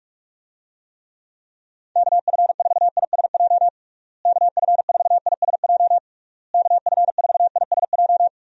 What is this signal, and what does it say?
Morse code, heard as a keyed tone, about 46 wpm, 700 Hz. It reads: KF4ISJ KF4ISJ KF4ISJ
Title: USA Call Signs - 6 Characters - Repeated 3x - 45wpm